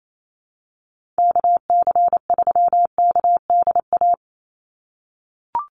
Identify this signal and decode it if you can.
Morse code, heard as a keyed tone, about 28 wpm, 700 Hz. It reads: KC3KDA
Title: USA Call Signs 28wpm